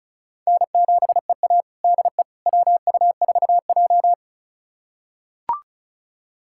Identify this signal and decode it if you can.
Morse code, heard as a keyed tone, about 36 wpm, 700 Hz. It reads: N7EA DE WU4J
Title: USA Callsign DE Callsign - 4 Characters - 35wpm